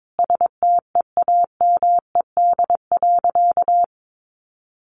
Morse code, heard as a keyed tone, about 22 wpm, 700 Hz. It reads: STEAMED.
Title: A Sound Of Thunder 22wpm